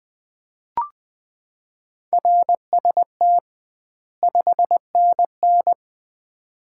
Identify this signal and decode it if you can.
Morse code, heard as a keyed tone, about 20 wpm, 700 Hz. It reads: RST 5NN
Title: Common QSO Phrases - 2x Word-Spacing - 20wpm